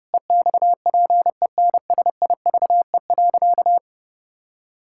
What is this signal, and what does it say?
Morse code, heard as a keyed tone, about 32 wpm, 700 Hz. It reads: EXPENSIVE.
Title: Pangram Challenge v2 - 30wpm